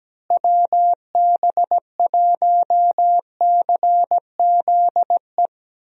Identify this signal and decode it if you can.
Morse code, heard as a keyed tone, about 17 wpm, 700 Hz. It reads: WB1CZE
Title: USA Call Signs - 6 Characters - Repeated 3x - 17wpm